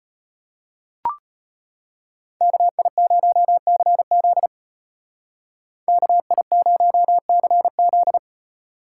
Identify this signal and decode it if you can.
Morse code, heard as a keyed tone, about 35 wpm, 700 Hz. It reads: KI0CZ KI0CZ